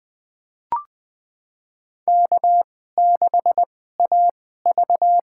Morse code, heard as a keyed tone, about 20 wpm, spaced as for 15 wpm, 700 Hz. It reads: K6AV
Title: USA Call Signs - 2x Character Spacing - 20wpm